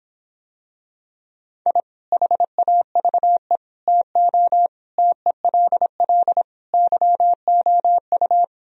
Morse code, heard as a keyed tone, about 26 wpm, 700 Hz. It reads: I HAVE TO TELL YOU